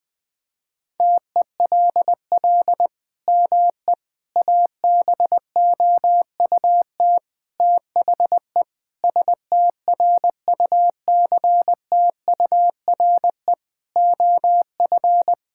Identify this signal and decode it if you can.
Morse code, heard as a keyed tone, about 20 wpm, 700 Hz. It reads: TELL ME ABOUT THE STRUCTURE OF
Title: Sentences from Top 1000 Words - Review Part 6 - 20wpm